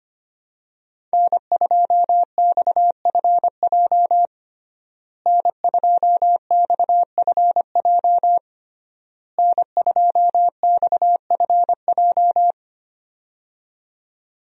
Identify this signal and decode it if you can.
Morse code, heard as a keyed tone, about 25 wpm, 700 Hz. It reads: N2XFJ N2XFJ N2XFJ
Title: USA Call Signs - 5 Characters - Repeated 3x - 25wpm